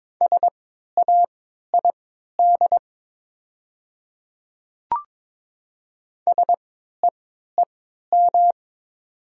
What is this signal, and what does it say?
Morse code, heard as a keyed tone, about 22 wpm, spaced as for 12 wpm, 700 Hz. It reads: SAID SEEM
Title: 3x Character Spacing - Top 300 Words - 22wpm